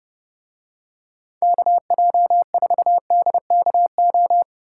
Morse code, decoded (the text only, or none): KJ4DKO